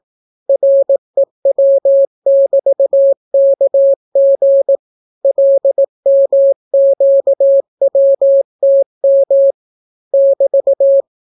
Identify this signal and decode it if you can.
Morse code, heard as a keyed tone, about 17 wpm, 550 Hz. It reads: REW<BT>KG LMQWTM <BT>